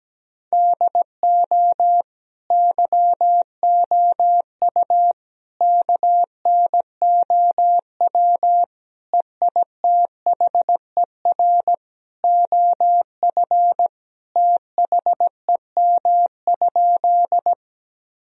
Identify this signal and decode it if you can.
Morse code, heard as a keyed tone, about 17 wpm, 700 Hz. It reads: DO YOU KNOW EITHER OF THEM?
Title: Sentences from Top 1000 Words - Review Part 4 - 17wpm